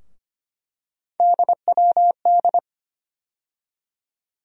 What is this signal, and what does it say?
Morse code, heard as a keyed tone, about 25 wpm, 700 Hz. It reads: DWD
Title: USA Call Sign Postfixes - 3 Letters 25wpm